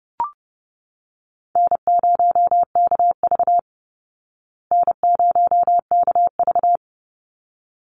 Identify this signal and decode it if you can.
Morse code, heard as a keyed tone, about 30 wpm, 700 Hz. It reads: N0KV N0KV
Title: Call Signs Repeated 3x - 30wpm